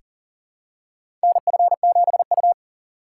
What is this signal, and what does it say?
Morse code, heard as a keyed tone, about 40 wpm, 700 Hz. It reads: NF7U
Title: USA Call Signs - 6x Character Spacing - 40wpm